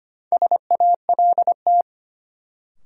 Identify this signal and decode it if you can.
Morse code, heard as a keyed tone, about 25 wpm, 700 Hz. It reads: SALT
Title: Sets of 5 Words - New Word Drill - 25wpm